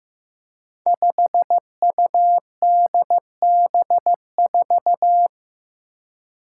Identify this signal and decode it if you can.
Morse code, heard as a keyed tone, about 15 wpm, 700 Hz. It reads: 5UDB4
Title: Five Challenging Letter-Number Combinations 15wpm